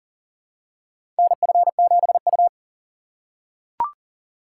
Morse code, heard as a keyed tone, about 40 wpm, 700 Hz. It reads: NF7U
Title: USA Call Signs - 4 Characters - 4x Character Spacing - 40wpm